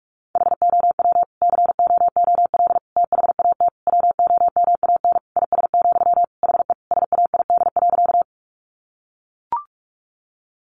Morse code, heard as a keyed tone, about 44 wpm, 700 Hz. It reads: HOW COOL THAT WOMAN IS, HE SAID.
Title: Sentences from Top 1200 Words 45wpm